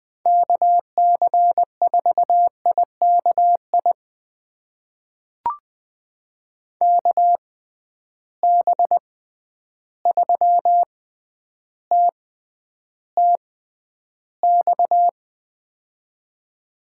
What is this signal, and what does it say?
Morse code, heard as a keyed tone, about 20 wpm, spaced as for 10 wpm, 700 Hz. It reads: KC4IKI KB3TTX
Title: USA Call Signs - 6 Characters - 6x Character Spacing - 20wpm